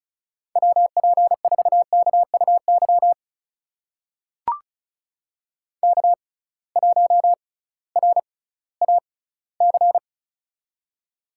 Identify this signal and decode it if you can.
Morse code, heard as a keyed tone, about 36 wpm, spaced as for 19 wpm, 700 Hz. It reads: WP4KUY K1RAC
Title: USA Call Signs - 6x Character Spacing - 35wpm